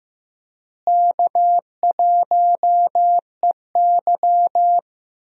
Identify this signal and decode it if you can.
Morse code, heard as a keyed tone, about 15 wpm, 700 Hz. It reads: K1EY
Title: USA Call Signs - 4 Characters - Repeated 2x - 15wpm